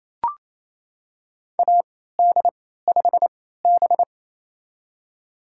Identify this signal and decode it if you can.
Morse code, heard as a keyed tone, about 28 wpm, spaced as for 16 wpm, 700 Hz. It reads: AD5B
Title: USA Call Signs - 4 Characters - 3x Character Spacing - 28wpm